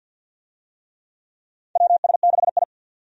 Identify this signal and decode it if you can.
Morse code, heard as a keyed tone, about 48 wpm, 700 Hz. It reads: WS6I
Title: USA Call Signs - 4 Characters - 50wpm